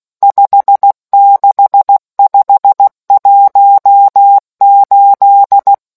Morse code, heard as a keyed tone, about 16 wpm, 800 Hz. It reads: 56518